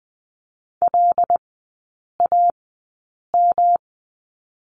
Morse code, heard as a keyed tone, about 20 wpm, spaced as for 8 wpm, 700 Hz. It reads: LAM
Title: 133 Lesson 22e - Question Mark - Three Characters 20wpm